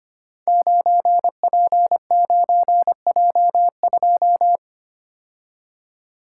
Morse code, heard as a keyed tone, about 25 wpm, 700 Hz. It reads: 9P9J2
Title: Code Groups - 5 Letter:Numbers - 25wpm